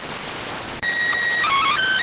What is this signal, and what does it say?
SSTV (slow-scan television), PD 290.